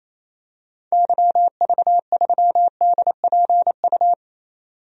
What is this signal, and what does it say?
Morse code, heard as a keyed tone, about 28 wpm, 700 Hz. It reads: YV3DPU
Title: DXCC-100 Call Signs - 6 Characters - 28wpm